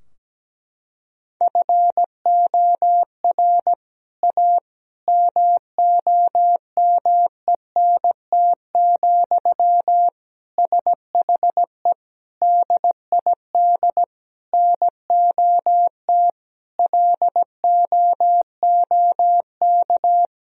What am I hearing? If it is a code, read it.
Morse code, heard as a keyed tone, about 17 wpm, 700 Hz. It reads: FOR A MOMENT, SHE DID NOT LOOK
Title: Sentences from Top 1000 Words - Review Part 2 - 17wpm